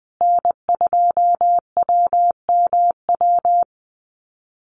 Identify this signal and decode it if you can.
Morse code, heard as a keyed tone, about 19 wpm, 700 Hz. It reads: N2WMW
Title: USA Call Signs Repeated 2x 20wpm